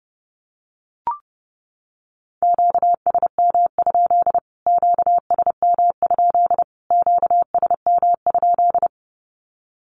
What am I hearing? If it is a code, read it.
Morse code, heard as a keyed tone, about 30 wpm, 700 Hz. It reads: QSM? QSM? QSM?
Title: Q-Codes Repeated 3x 30wpm